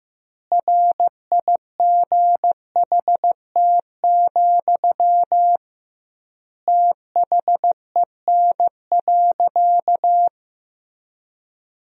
Morse code, heard as a keyed tone, about 15 wpm, 700 Hz. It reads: RIGHT, THEN.
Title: 2x Word-Spacing - Sentences from Top 200 Words - 15wpm